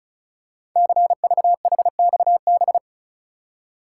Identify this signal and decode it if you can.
Morse code, heard as a keyed tone, about 36 wpm, 690 Hz. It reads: CVHXB
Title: Code Groups - 5 Letters - 35wpm